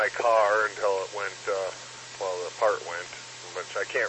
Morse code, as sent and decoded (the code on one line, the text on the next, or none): none